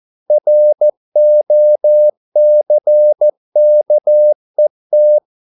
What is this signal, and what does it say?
Morse code, heard as a keyed tone, about 14 wpm, 600 Hz. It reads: ROCKET